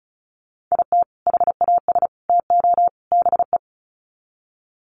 Morse code, heard as a keyed tone, about 34 wpm, 700 Hz. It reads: IT HAS TO BE
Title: Sets of 4 Words 35wpm